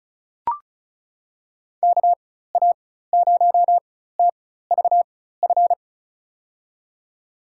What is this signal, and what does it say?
Morse code, heard as a keyed tone, about 36 wpm, spaced as for 16 wpm, 700 Hz. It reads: KA0TVF
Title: USA Call Signs - 6 Characters - 4x Character Spacing - 35wpm